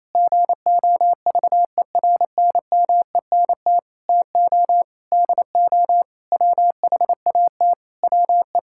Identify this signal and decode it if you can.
Morse code, heard as a keyed tone, about 28 wpm, 700 Hz. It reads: GOVERNMENT TO DO WHAT WE
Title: Sentences from Top 1600 Words 28wpm